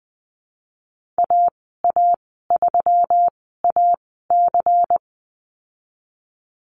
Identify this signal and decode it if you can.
Morse code, heard as a keyed tone, about 20 wpm, spaced as for 15 wpm, 700 Hz. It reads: AA3AC
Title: 128 Lesson 21g - Number 6 - Call Signs 20wpm